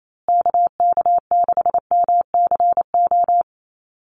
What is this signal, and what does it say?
Morse code, heard as a keyed tone, about 28 wpm, 700 Hz. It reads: KK6MCO
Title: USA Call Signs Repeated 3x 28wpm